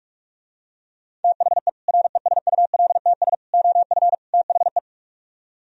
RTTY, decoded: none

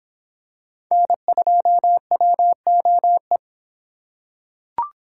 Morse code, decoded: N2WOE

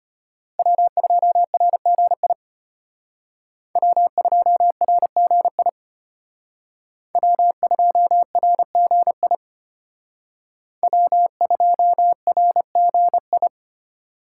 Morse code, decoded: W2RGI W2RGI W2RGI W2RGI

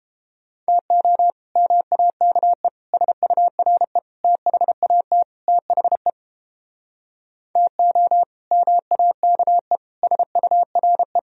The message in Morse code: TO MAKE SURE THAT THE TO MAKE SURE

